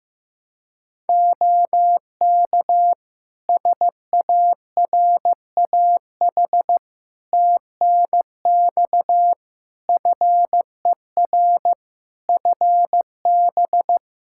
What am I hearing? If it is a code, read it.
Morse code, heard as a keyed tone, about 16 wpm, 700 Hz. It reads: OK SARAH TNX FER FB
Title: Practice QSO - Exchange 4 - Ending - 15wpm